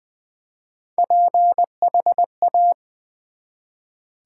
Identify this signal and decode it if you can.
Morse code, heard as a keyed tone, about 20 wpm, 700 Hz. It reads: PHA
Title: Build A Word - 1 Letter At A Time - 5-Letter Words - 20wpm